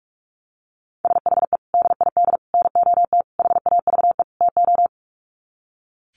Morse code, heard as a keyed tone, about 44 wpm, 700 Hz. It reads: SHE DID NOT HAVE TO
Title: Sets of 5 Words Encore2 45wpm